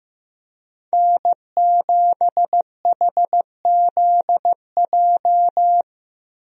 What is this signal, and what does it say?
Morse code, heard as a keyed tone, about 14 wpm, 700 Hz. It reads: N7HZJ
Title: USA Call Signs - 5 Characters - Repeated 3x - 15wpm